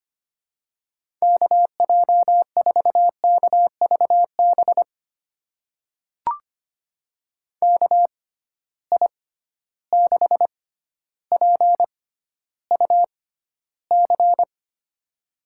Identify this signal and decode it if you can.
Morse code, heard as a keyed tone, about 25 wpm, spaced as for 13 wpm, 700 Hz. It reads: KJ4KVB KI6PUC
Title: USA Call Signs - 6 Characters - 6x Character Spacing - 25wpm